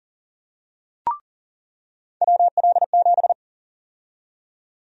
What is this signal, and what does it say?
Morse code, heard as a keyed tone, about 40 wpm, 700 Hz. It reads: WL7
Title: USA Call Sign Prefixes - 40wpm